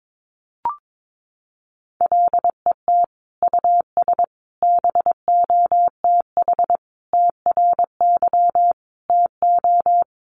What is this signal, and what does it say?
Morse code, heard as a keyed tone, about 22 wpm, 700 Hz. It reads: LET US BOTH TRY TO
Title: Sentences from Top 1200 Words 22wpm